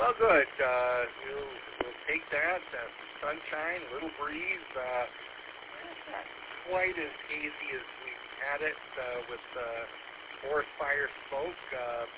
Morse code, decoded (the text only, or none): none